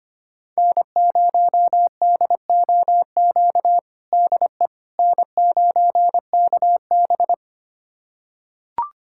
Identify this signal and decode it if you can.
Morse code, heard as a keyed tone, about 25 wpm, 700 Hz. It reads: N0DOQ DE N9KB
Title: USA Callsign DE Callsign 25wpm